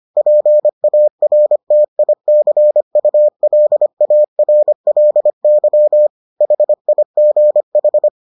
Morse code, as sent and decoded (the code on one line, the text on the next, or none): .--. .- .-. - .. -.-. ..- .-.. .- .-. .-.. -.-- / .... .. --. ....
PARTICULARLY HIGH